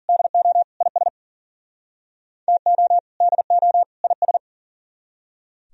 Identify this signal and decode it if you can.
Morse code, heard as a keyed tone, about 51 wpm, 700 Hz. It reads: DO IS TO DO IS